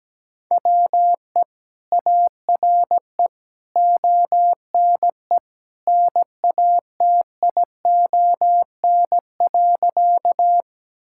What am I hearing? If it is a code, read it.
Morse code, heard as a keyed tone, about 17 wpm, 700 Hz. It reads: WE ARE ONE NATION.
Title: Sentences from Top 1000 Words - Review Part 6 - 17wpm